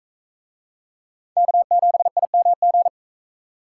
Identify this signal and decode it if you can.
Morse code, heard as a keyed tone, about 39 wpm, 700 Hz. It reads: K7IMG